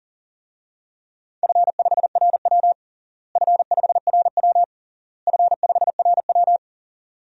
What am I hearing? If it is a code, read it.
Morse code, heard as a keyed tone, about 40 wpm, 700 Hz. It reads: F5RW F5RW F5RW
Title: DXCC-100 Call Signs - 4 Characters - Repeated 3x - 40wpm